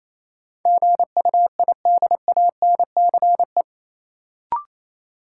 Morse code, heard as a keyed tone, about 28 wpm, 700 Hz. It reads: GUIDANCE
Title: Sentences from Top 1600 Words - New Word Drill - 28wpm